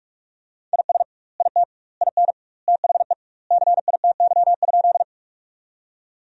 Morse code, heard as a keyed tone, about 46 wpm, 700 Hz. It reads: IS IT IN THE CITY?
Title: 2x Word-Spacing - Sentences from Top 300 Words - 45wpm